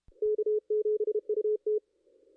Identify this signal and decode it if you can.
Morse code, heard as a keyed tone, about 30 wpm, 440 Hz. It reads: K7UT
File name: contests / ss-cw-k5tr-op-n5rz -2200